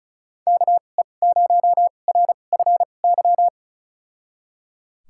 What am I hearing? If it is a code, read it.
Morse code, heard as a keyed tone, about 36 wpm, spaced as for 25 wpm, 700 Hz. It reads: KE0RFY
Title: USA Call Signs - 6 Characters - 2x Character Spacing - 35wpm